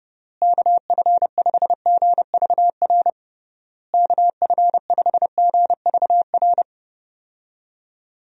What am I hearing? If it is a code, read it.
Morse code, heard as a keyed tone, about 29 wpm, 700 Hz. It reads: KF5GVR KF5GVR
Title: USA Call Signs - 6 Characters - Repeated 3x - 30wpm